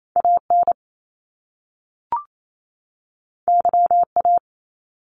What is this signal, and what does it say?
Morse code, heard as a keyed tone, about 28 wpm, 700 Hz. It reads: AN YA
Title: Top Two Letter Combinations in Words 28wpm